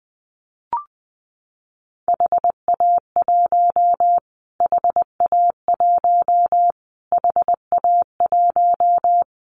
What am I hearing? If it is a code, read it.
Morse code, heard as a keyed tone, about 20 wpm, 700 Hz. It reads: HA1 HA1 HA1
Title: Callsign Prefixes for Top 25 DXCC - Repeated 3x - 20wpm